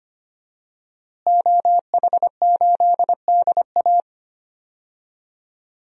Morse code, heard as a keyed tone, about 25 wpm, 700 Hz. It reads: OH8DA
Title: DXCC-100 Call Signs - 5 Characters - Repeated 2x - 25wpm